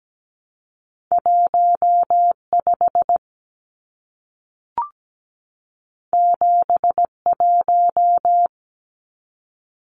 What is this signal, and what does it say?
Morse code, heard as a keyed tone, about 17 wpm, 700 Hz. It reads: 15 71
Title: Numbers - 2-Digit 17wpm